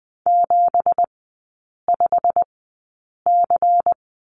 Morse code, heard as a keyed tone, about 20 wpm, spaced as for 8 wpm, 690 Hz. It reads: 75C
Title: 166 Lesson 27e - Number 7 - Three Characters 20wpm